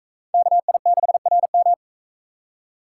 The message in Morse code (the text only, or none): KI6RM